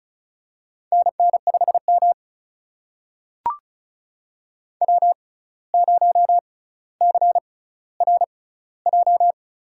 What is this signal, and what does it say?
Morse code, heard as a keyed tone, about 36 wpm, spaced as for 17 wpm, 700 Hz. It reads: NN5M W0CRJ
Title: USA Call Signs - 6x Character Spacing - 35wpm